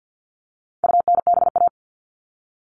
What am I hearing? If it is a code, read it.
Morse code, heard as a keyed tone, about 50 wpm, 700 Hz. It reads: UNBA
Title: All Cut Numbers - 4-Digit 50wpm